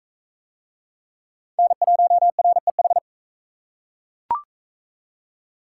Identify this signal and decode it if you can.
Morse code, heard as a keyed tone, about 39 wpm, 700 Hz. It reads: N1REH